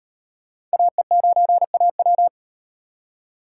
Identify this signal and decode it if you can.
Morse code, heard as a keyed tone, about 38 wpm, 700 Hz. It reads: AE9AW